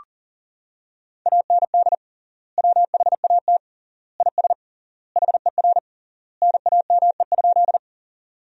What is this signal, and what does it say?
Morse code, heard as a keyed tone, about 37 wpm, 700 Hz. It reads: AND WHAT IS HER NAME?
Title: 3x Word-Spacing - Sentences from Top 200 Words - 40wpm